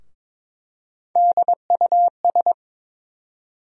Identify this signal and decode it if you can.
Morse code, heard as a keyed tone, about 21 wpm, 700 Hz. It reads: DUS
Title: USA Call Sign Postfixes - 3 Letters 22wpm